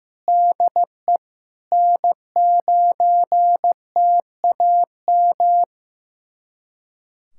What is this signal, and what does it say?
Morse code, heard as a keyed tone, about 16 wpm, 700 Hz. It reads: DE N9TAM
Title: USA Callsign DE Callsign - 5 Characters - 15wpm